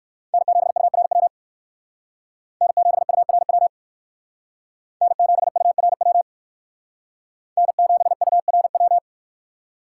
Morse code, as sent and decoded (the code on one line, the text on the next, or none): -. --... ..- .-. .-- / -. --... ..- .-. .-- / -. --... ..- .-. .-- / -. --... ..- .-. .--
N7URW N7URW N7URW N7URW